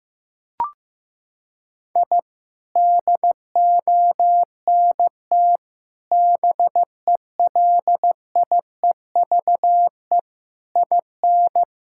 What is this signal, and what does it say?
Morse code, heard as a keyed tone, about 16 wpm, 700 Hz. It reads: I DONT BELIEVE IN
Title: Sentences from Top 1000 Words - Review Part 6 - 15wpm